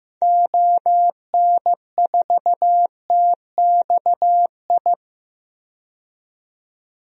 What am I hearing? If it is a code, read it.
Morse code, heard as a keyed tone, about 15 wpm, 700 Hz. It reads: ON4TXI
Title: DXCC-100 Call Signs - 6 Characters - 15wpm